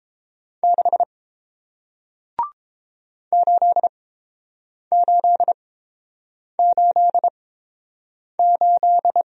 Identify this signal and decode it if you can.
Morse code, heard as a keyed tone, about 30 wpm, spaced as for 7 wpm, 700 Hz. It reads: 6 8888